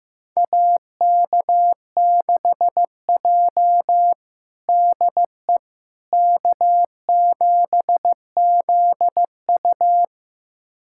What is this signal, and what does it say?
Morse code, heard as a keyed tone, about 15 wpm, 700 Hz. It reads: AK6J DE K7ZU
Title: USA Callsign DE Callsign - 4 Characters - 15wpm